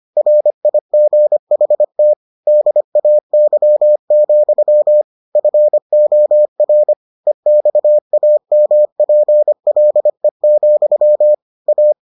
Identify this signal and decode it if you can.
Morse code, heard as a keyed tone, about 25 wpm, 600 Hz. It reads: RIGHT DAY, FOR EXAMPLE, A